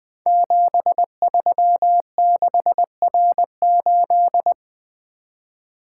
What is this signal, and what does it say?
Morse code, heard as a keyed tone, about 20 wpm, 700 Hz. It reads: 736R8
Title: Code Groups - 5 Letter:Numbers - 20wpm